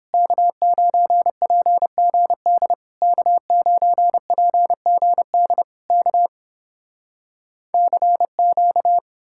Morse code, heard as a keyed tone, about 30 wpm, 700 Hz. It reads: K9PGD K9PGD K CQ